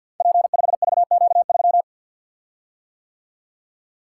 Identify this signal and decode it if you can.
Morse code, heard as a keyed tone, about 53 wpm, 700 Hz. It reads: P5VQ3
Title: Code Groups - 5 Letter:Numbers - 50wpm